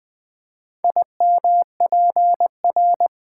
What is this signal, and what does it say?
Morse code, heard as a keyed tone, about 20 wpm, 700 Hz. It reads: IMPR
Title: Build A Word - 2 Letters At A Time - 8-Letter Words - 20wpm